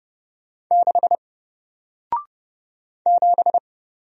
Morse code, heard as a keyed tone, about 30 wpm, 700 Hz. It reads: 6 7
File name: Single-Letter-Number-30wpm